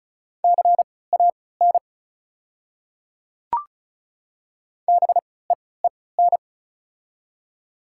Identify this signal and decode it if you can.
Morse code, heard as a keyed tone, about 36 wpm, spaced as for 20 wpm, 690 Hz. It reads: CAN BEEN
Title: 3x Character Spacing - Top 100 Words - 35wpm